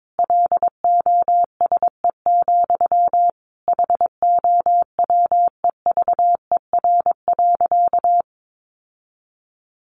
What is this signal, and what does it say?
Morse code, heard as a keyed tone, about 22 wpm, 700 Hz. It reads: LOSE, HOWEVER.
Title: Sentences from Top 1100 Words 22wpm